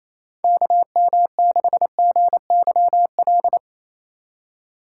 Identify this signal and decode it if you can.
Morse code, heard as a keyed tone, about 28 wpm, 700 Hz. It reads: KM6GYL
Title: USA Call Signs - 6 Characters - 2x Character Spacing - 28wpm